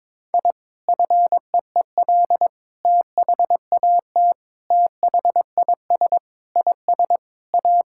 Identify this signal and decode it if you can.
Morse code, heard as a keyed tone, about 22 wpm, 700 Hz. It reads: I FEEL THAT THIS IS A
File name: practice / 22wpm / Sentences with 4-Word Phrases - 22wpm